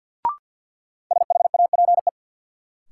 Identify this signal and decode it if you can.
Morse code, heard as a keyed tone, about 51 wpm, 700 Hz. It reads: SHAPE
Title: Sentences from Top 500 Words - New Word Drill - 50wpm